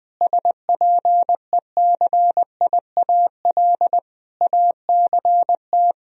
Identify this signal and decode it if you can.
Morse code, heard as a keyed tone, about 20 wpm, 700 Hz. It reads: SPECIAL ACT
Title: Sentences from Top 1000 Words - Review Part 3 - 20wpm